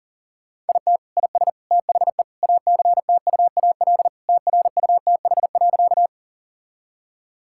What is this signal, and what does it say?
Morse code, heard as a keyed tone, about 40 wpm, 700 Hz. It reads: IT IS THE ACTUAL TRUTH.
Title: Sentences from Top 1000 Words - Review Part 5 - 40wpm